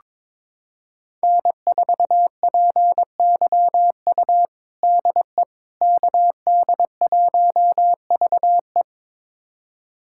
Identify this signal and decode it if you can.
Morse code, heard as a keyed tone, about 22 wpm, 700 Hz. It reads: N4PYU DE KD1VE
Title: USA Callsign DE Callsign - 5 Characters - 22wpm